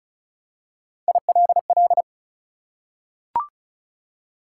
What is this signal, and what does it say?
Morse code, heard as a keyed tone, about 36 wpm, 700 Hz. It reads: ILL
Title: Three Letter Words - 35wpm